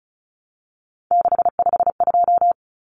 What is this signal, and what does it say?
Morse code, heard as a keyed tone, about 34 wpm, 700 Hz. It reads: 652
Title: Common Cut Numbers - 3-Digit 35wpm